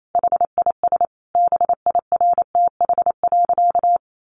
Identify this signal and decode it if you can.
Morse code, heard as a keyed tone, about 28 wpm, 700 Hz. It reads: HIS BIRTH.
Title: Sentences from Top 900 Words 28wpm